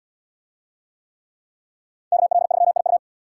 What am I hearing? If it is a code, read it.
Morse code, heard as a keyed tone, about 72 wpm, 700 Hz. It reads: KL3IU